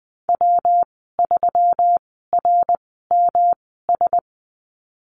Morse code, heard as a keyed tone, about 20 wpm, spaced as for 15 wpm, 700 Hz. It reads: W3RMS
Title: 154 Lesson 25g - Letter p - Call Signs 20wpm